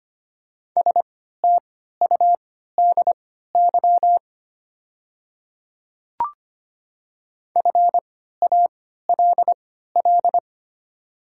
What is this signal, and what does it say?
Morse code, heard as a keyed tone, about 25 wpm, spaced as for 14 wpm, 690 Hz. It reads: STUDY FALL
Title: 3x Character Spacing - Top 500 Words - 25wpm